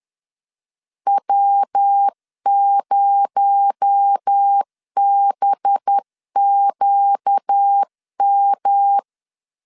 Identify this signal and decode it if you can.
Morse code, heard as a keyed tone, about 10 wpm, 800 Hz. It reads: W0BQM
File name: LEZIONI / 7 GF9Z / 7 GF 10wpm 2514